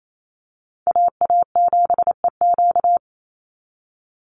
Morse code, heard as a keyed tone, about 28 wpm, 700 Hz. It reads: AA7EQ
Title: NA SOTA Chaser Callsigns 28wpm